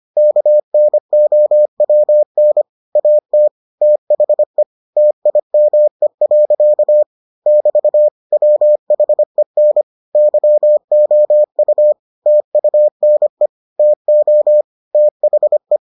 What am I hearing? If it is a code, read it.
Morse code, heard as a keyed tone, about 25 wpm, 600 Hz. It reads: KNOWN AT THE TIME. <BT> WHEN YOU TUNE TO THE